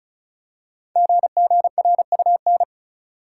Morse code, heard as a keyed tone, about 36 wpm, 700 Hz. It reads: GGRUN
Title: Code Groups - 5 Letters - 35wpm